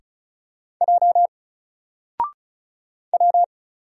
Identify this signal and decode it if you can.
Morse code, heard as a keyed tone, about 34 wpm, 700 Hz. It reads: J W
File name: Single-Letter-Number-35wpm